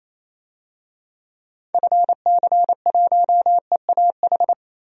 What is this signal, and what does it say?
Morse code, heard as a keyed tone, about 28 wpm, 700 Hz. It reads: FC1EAH
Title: DXCC-100 Call Signs - 6 Characters - 28wpm